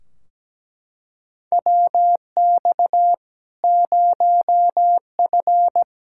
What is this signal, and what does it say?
Morse code, heard as a keyed tone, about 17 wpm, 700 Hz. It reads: WX 0F
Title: Practice QSO - Exchange 2 - Info - 17wpm